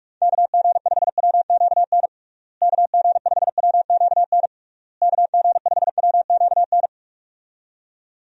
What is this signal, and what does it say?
Morse code, heard as a keyed tone, about 41 wpm, 700 Hz. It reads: KG5WQN KG5WQN KG5WQN
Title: USA Call Signs - 6 Characters - Repeated 3x - 45wpm